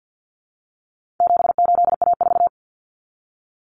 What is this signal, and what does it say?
Morse code, heard as a keyed tone, about 50 wpm, 700 Hz. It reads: 78A4
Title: Common Cut Numbers - 4-Digit 50wpm